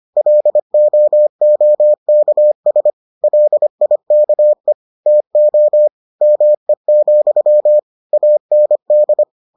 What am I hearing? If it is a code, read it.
Morse code, heard as a keyed tone, about 25 wpm, 600 Hz. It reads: LOOKS LIKE TO ME, AND